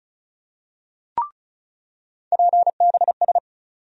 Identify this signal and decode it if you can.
Morse code, heard as a keyed tone, about 36 wpm, 700 Hz. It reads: PBS
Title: USA Call Sign Postfixes - 3 Letters 35wpm